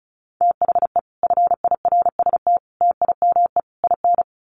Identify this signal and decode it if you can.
Morse code, heard as a keyed tone, about 34 wpm, 700 Hz. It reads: THE FIRST TIME IN
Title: Sets of 4 Words 35wpm